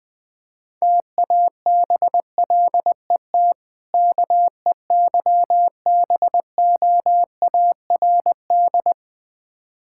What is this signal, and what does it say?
Morse code, heard as a keyed tone, about 21 wpm, 700 Hz. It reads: TABLET KEYBOARD
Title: Difficult Words with Hints - 20wpm